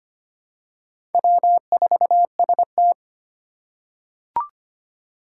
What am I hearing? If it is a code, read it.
Morse code, heard as a keyed tone, about 25 wpm, 700 Hz. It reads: W4ST
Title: Four Challenging Letter-Number Combinations 25wpm